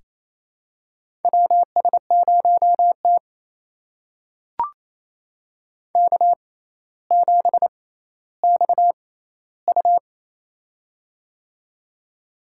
Morse code, heard as a keyed tone, about 27 wpm, spaced as for 14 wpm, 700 Hz. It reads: WS0T K7XU
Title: USA Call Signs - 4 Characters - 6x Character Spacing - 28wpm